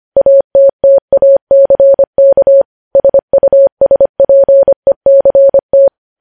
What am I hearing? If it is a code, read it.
Morse code, heard as a keyed tone, about 25 wpm, 550 Hz. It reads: ATTACK SUSPECT
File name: podcast / News headlines-1x-25x25